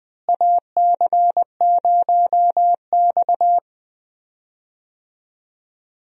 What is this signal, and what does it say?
Morse code, heard as a keyed tone, about 20 wpm, 700 Hz. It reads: AC0X
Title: USA Call Signs - 4 Characters - Repeated 3x - 20wpm